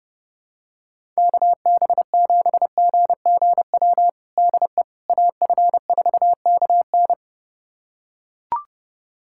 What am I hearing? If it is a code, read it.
Morse code, heard as a keyed tone, about 30 wpm, 700 Hz. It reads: KB7GGW DE AF4KN
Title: USA Callsign DE Callsign 30wpm